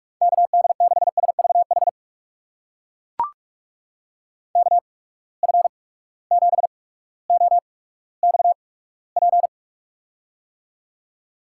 Morse code, heard as a keyed tone, about 42 wpm, spaced as for 19 wpm, 700 Hz. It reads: KD6SVH KF7OXP
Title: USA Call Signs - 6 Characters - 8x Character Spacing - 45wpm